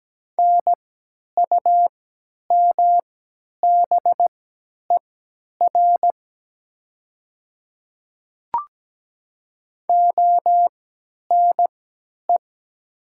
Morse code, heard as a keyed tone, about 17 wpm, spaced as for 10 wpm, 700 Hz. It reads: NUMBER ONE
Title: 3x Character Spacing - Top 100 Words - 17wpm